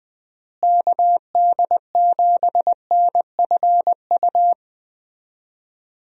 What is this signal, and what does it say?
Morse code, heard as a keyed tone, about 20 wpm, 700 Hz. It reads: KD7NFU
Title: USA Call Signs - 6 Characters - Repeated 2x - 20wpm